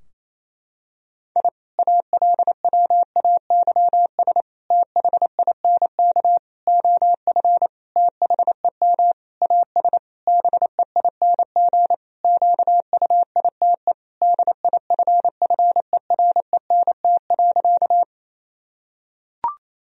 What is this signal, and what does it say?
Morse code, heard as a keyed tone, about 28 wpm, 700 Hz. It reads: I ALWAYS THINK OF THEM AS BEING QUITE DIFFERENT.
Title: Sentences from Top 1000 Words - Review Part 3 - 28wpm